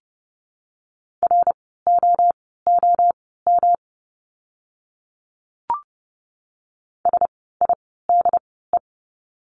Morse code, heard as a keyed tone, about 30 wpm, spaced as for 17 wpm, 700 Hz. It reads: ROOM SIDE
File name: course / 30:10wpm / 115 Lesson 19f - Letter w - Words 30wpm